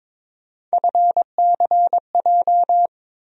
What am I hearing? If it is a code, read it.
Morse code, heard as a keyed tone, about 22 wpm, 700 Hz. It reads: FCJ
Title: USA Call Sign Postfixes - 3 Letters 22wpm